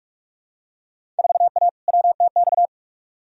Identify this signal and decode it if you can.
Morse code, heard as a keyed tone, about 48 wpm, 700 Hz. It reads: 4A WTX
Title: Field Day Exchanges Encore - 45wpm